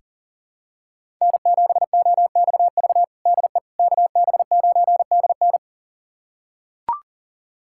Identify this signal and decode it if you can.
Morse code, heard as a keyed tone, about 37 wpm, 700 Hz. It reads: N7OXV DE KB9DN
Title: USA Callsign DE Callsign - 5 Characters - 40wpm